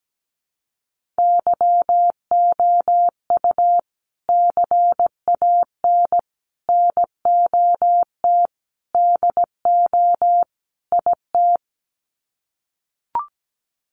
Morse code, heard as a keyed tone, about 17 wpm, 700 Hz. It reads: YOU CAN NOT DO IT
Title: Sets of 5 Words Encore2 17wpm